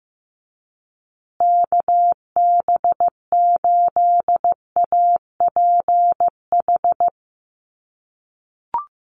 Morse code, heard as a keyed tone, about 15 wpm, 700 Hz. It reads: KB8APH
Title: USA Call Signs Repeated 3x 15wpm